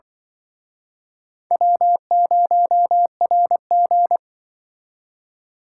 Morse code, heard as a keyed tone, about 23 wpm, 700 Hz. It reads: W0RG
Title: Speed Racing - USA 1x2 Call Signs 20wpm